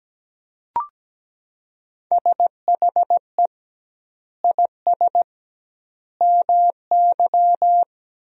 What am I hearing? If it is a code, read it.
Morse code, heard as a keyed tone, about 17 wpm, 700 Hz. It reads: SHE IS MY
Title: 2x Word-Spacing - Sentences from Top 300 Words - 17wpm